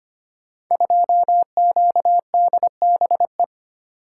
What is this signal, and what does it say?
Morse code, heard as a keyed tone, about 25 wpm, 700 Hz. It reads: 2QDBE
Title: Code Groups - 5 Letter:Numbers - 25wpm